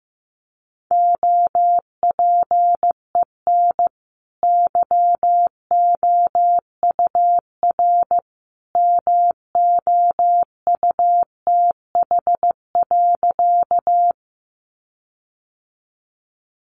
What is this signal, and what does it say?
Morse code, heard as a keyed tone, about 15 wpm, 700 Hz. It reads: OPEN YOUR MOUTH.
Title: Sentences from Top 1500 Words 15wpm